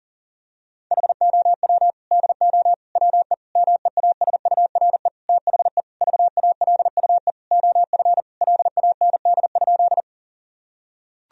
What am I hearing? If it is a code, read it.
Morse code, heard as a keyed tone, about 40 wpm, 700 Hz. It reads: HOW DO WE MEASURE THE VALUE OF LAND?
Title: Sentences from Top 1000 Words - Review Part 3 - 40wpm